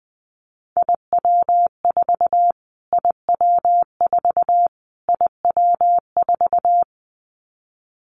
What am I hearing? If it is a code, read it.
Morse code, heard as a keyed tone, about 20 wpm, 700 Hz. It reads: IW4 IW4 IW4
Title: Callsign Prefixes for Top 25 DXCC - Repeated 3x - 20wpm